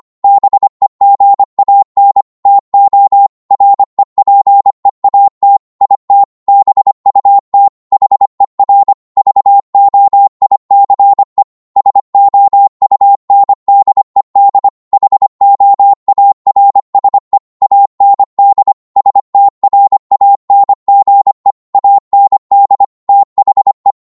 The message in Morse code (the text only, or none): BEGAN TO REPEAT IT BUT HER VOICE SOUNDED HOARSE AND STRANGE AND THE